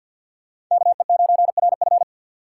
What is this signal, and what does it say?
Morse code, heard as a keyed tone, about 51 wpm, 700 Hz. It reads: KE9RF
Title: USA Call Signs - 5 Characters - 4x Character Spacing - 50wpm